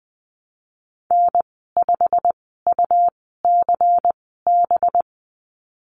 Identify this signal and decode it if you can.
Morse code, heard as a keyed tone, about 20 wpm, spaced as for 15 wpm, 700 Hz. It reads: N5UCB
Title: 193 Lesson 31g - Letter v - Call Signs 20wpm